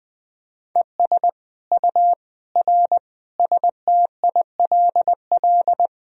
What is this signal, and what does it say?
Morse code, heard as a keyed tone, about 20 wpm, 700 Hz. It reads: ES U R STILL
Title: Sentences from Top 400 Words and Abbreviations 20wpm